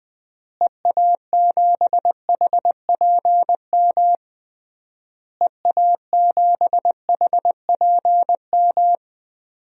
Morse code, heard as a keyed tone, about 20 wpm, 700 Hz. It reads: EA7HPM EA7HPM
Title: DXCC-100 Call Signs - 6 Characters - Repeated 3x - 20wpm